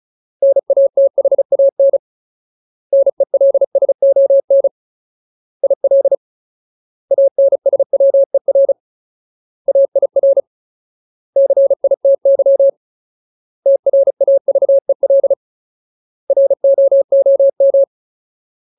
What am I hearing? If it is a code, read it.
Morse code, heard as a keyed tone, about 34 wpm, 550 Hz. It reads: NATHAN NELSON IL ANSWER AIR CITY TRAVEL ROOM
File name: Ultimate Word List / 35WPM Ultimate